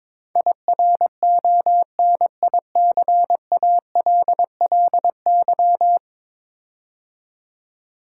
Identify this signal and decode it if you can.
Morse code, heard as a keyed tone, about 22 wpm, 700 Hz. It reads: IRONICALLY
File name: practice / 22wpm / 10 Letter Words - First 3 Letters Given - 22wpm